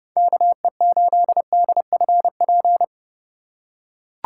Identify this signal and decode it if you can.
Morse code, heard as a keyed tone, about 32 wpm, 700 Hz. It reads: KE8DFP
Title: USA Call Signs - 6 Characters - 3x Character Spacing - 30wpm